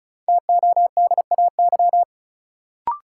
Morse code, heard as a keyed tone, about 36 wpm, 700 Hz. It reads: TODAY
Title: Sentences from Top 600 Words - New Word Drill - 35wpm